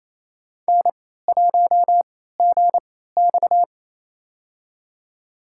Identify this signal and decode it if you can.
Morse code, heard as a keyed tone, about 27 wpm, spaced as for 16 wpm, 700 Hz. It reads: N1GX
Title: USA Call Signs - 4 Characters - 3x Character Spacing - 28wpm